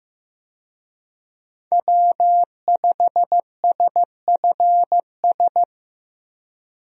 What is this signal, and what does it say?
Morse code, heard as a keyed tone, about 15 wpm, 700 Hz. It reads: W5SFS
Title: USA Call Signs - 5 Characters - 15wpm